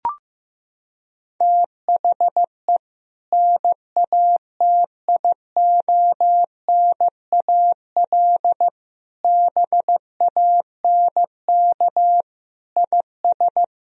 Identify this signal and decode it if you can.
Morse code, heard as a keyed tone, about 15 wpm, 700 Hz. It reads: THE NATIONAL BANK IS
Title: Sentences from Top 1000 Words - Review Part 4 - 15wpm